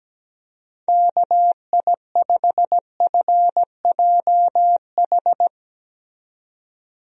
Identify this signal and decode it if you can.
Morse code, heard as a keyed tone, about 17 wpm, 700 Hz. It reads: KI5FJH
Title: USA Call Signs - 6 Characters - 17wpm